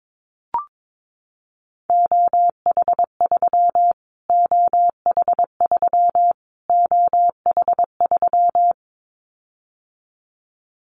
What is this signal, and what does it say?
Morse code, heard as a keyed tone, about 22 wpm, 700 Hz. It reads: OH3 OH3 OH3
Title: Callsign Prefixes for Top 25 DXCC - Repeated 3x - 22wpm